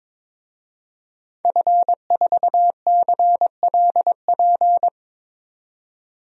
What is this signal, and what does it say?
Morse code, heard as a keyed tone, about 21 wpm, 700 Hz. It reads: F4CLP
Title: DXCC-100 Call Signs - 5 Characters - 22wpm